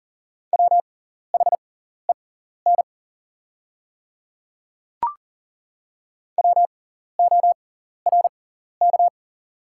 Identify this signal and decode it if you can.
Morse code, heard as a keyed tone, about 41 wpm, spaced as for 14 wpm, 690 Hz. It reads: WHEN WORK
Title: 6x Character Spacing - Top 300 Words - 40wpm